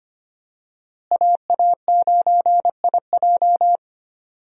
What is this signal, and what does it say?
Morse code, heard as a keyed tone, about 25 wpm, 700 Hz. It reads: AA9IJ
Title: USA Call Signs - 5 Characters - Repeated 2x - 25wpm